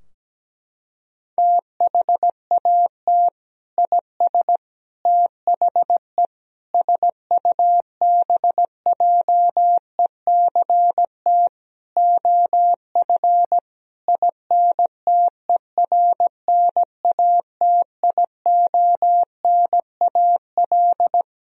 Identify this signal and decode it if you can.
Morse code, heard as a keyed tone, about 17 wpm, 700 Hz. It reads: THAT IS THE SUBJECT OF INTERNATIONAL
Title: Sentences from Top 1000 Words - Review Part 4 - 17wpm